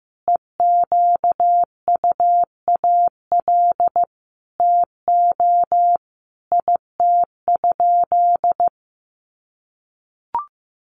Morse code, heard as a keyed tone, about 14 wpm, 700 Hz. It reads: EQUAL TO IT?
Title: Sentences from Top 1400 Words 15wpm